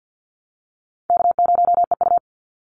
Morse code, heard as a keyed tone, about 48 wpm, 700 Hz. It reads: K0EU
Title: Call Signs Repeated 3x - 50wpm